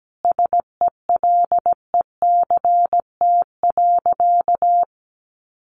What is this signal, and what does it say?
Morse code, heard as a keyed tone, about 17 wpm, 700 Hz. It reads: SELECT.
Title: Sentences from Top 1300 Words 17wpm